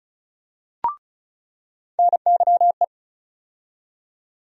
Morse code, heard as a keyed tone, about 36 wpm, 700 Hz. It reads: NYE
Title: USA Call Sign Postfixes - 3 Letters 35wpm